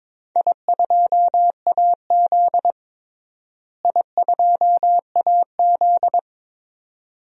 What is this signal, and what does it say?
Morse code, heard as a keyed tone, about 22 wpm, 700 Hz. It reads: I2AZ I2AZ
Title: DXCC-100 Call Signs - 4 Characters - Repeated 2x - 22wpm